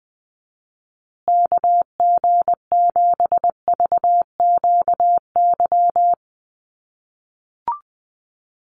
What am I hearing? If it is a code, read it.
Morse code, heard as a keyed tone, about 20 wpm, 700 Hz. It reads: KG7VQY
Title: USA Call Signs Repeated 3x 20wpm